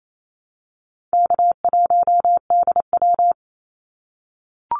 Morse code, heard as a keyed tone, about 28 wpm, 700 Hz. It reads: K1DW
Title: Easy Call Signs - 28wpm